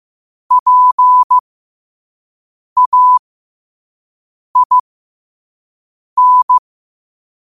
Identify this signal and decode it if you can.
Morse code, heard as a keyed tone, about 15 wpm, spaced as for 5 wpm, 1000 Hz. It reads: PAIN